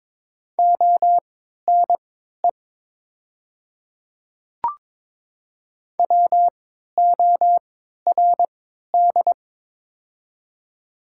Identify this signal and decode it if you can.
Morse code, heard as a keyed tone, about 21 wpm, spaced as for 12 wpm, 690 Hz. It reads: ONE WORD
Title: 3x Character Spacing - Top 100 Words - 22wpm